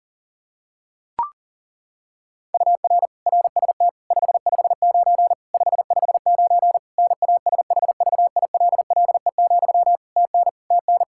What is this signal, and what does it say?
Morse code, heard as a keyed tone, about 40 wpm, 700 Hz. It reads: UR RST 559 559 NASHVILLE, TN TN
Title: Practice QSO - Exchange 1 - Beginning - 40wpm